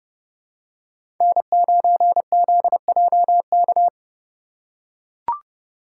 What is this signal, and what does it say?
Morse code, heard as a keyed tone, about 30 wpm, 700 Hz. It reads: N9ZJK
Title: USA Call Signs - 4x Character Spacing - 30wpm